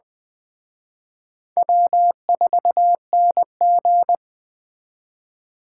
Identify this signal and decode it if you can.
Morse code, heard as a keyed tone, about 21 wpm, 700 Hz. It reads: W4NG